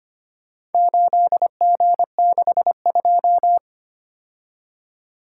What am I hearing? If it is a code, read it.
Morse code, heard as a keyed tone, about 25 wpm, 700 Hz. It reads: 8G62
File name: practice / 25wpm / Four Letter-Number Combinations - 25wpm